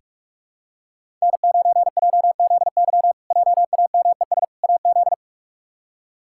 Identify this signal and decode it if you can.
Morse code, heard as a keyed tone, about 47 wpm, 700 Hz. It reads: N9JZY JAMES AZ
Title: K1USN SST QSO Practice - 45wpm